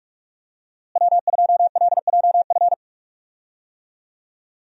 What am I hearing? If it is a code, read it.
Morse code, heard as a keyed tone, about 46 wpm, 700 Hz. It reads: W2LJF
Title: USA Call Signs - 5 Characters - Repeated 2x - 45wpm